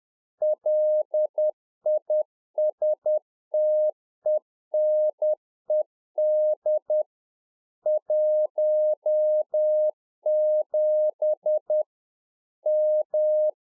Morse code, heard as a keyed tone, about 10 wpm, 610 Hz. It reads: LISTENED 17 M
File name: 2011-04-mp3 / UNFOGETTABLE QSO 10WPM